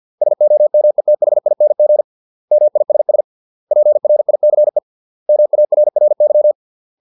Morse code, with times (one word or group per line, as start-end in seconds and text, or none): 0.2-2.0 SOMETHING
2.5-3.2 MISS
3.7-4.8 PRICE
5.3-6.5 MARRY